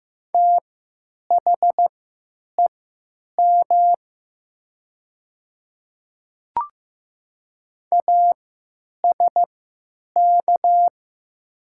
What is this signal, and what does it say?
Morse code, heard as a keyed tone, about 15 wpm, spaced as for 9 wpm, 700 Hz. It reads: THEM ASK
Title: 3x Character Spacing - Top 200 Words - 15wpm